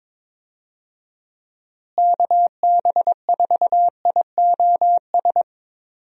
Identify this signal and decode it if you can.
Morse code, heard as a keyed tone, about 22 wpm, 700 Hz. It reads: KB4IOS
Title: USA Call Signs - 6 Characters - Repeated 2x - 22wpm